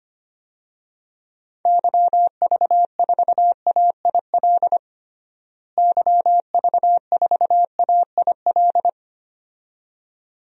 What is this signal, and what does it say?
Morse code, heard as a keyed tone, about 25 wpm, 700 Hz. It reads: YV4AIL YV4AIL
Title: DXCC-100 Call Signs - 6 Characters - Repeated 2x - 25wpm